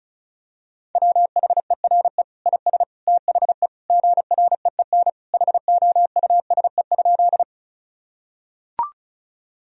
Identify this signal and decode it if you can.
Morse code, heard as a keyed tone, about 36 wpm, 700 Hz. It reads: WHERE IS THE GREEN HOUSE?
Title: Sentences from Top 1000 Words - Review Part 2 - 35wpm